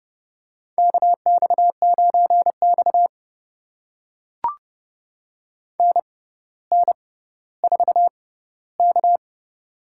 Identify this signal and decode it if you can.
Morse code, heard as a keyed tone, about 31 wpm, spaced as for 15 wpm, 700 Hz. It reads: KX9X NN4K
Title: USA Call Signs - 4 Characters - 6x Character Spacing - 30wpm